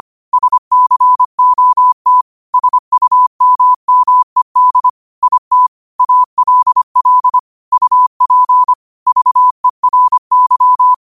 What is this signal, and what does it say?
Morse code, heard as a keyed tone, about 25 wpm, 1000 Hz. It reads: SCOT SUMMED IT ALL UP VERY
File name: qotdfiles / 25wpm